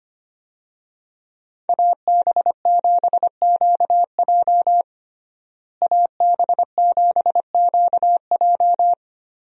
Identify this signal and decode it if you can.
Morse code, heard as a keyed tone, about 25 wpm, 700 Hz. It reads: AB7QJ AB7QJ
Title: USA Call Signs - 5 Characters - Repeated 2x - 25wpm